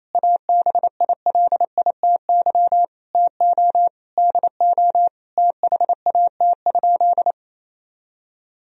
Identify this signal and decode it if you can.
Morse code, heard as a keyed tone, about 28 wpm, 700 Hz. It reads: ABILITY TO DO THAT?
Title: Sentences from Top 1000 Words - Review Part 3 - 28wpm